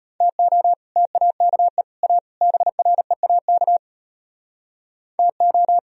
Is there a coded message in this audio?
Yes: Morse code from 0.2 to 5.8 s, heard as a keyed tone, about 38 wpm, 700 Hz. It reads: TO TAKE A BREAK TO